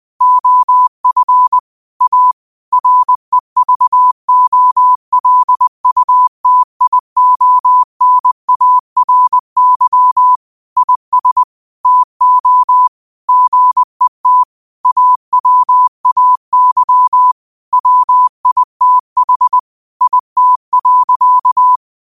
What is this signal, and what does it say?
Morse code, heard as a keyed tone, about 20 wpm, 1000 Hz. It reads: OF A REVOLUTIONARY IS TO GET AWAY WITH IT.